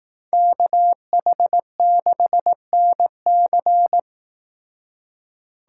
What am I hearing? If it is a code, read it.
Morse code, heard as a keyed tone, about 19 wpm, 700 Hz. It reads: KH6NC
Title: Speed Racing - USA 2x2 Call Signs 15wpm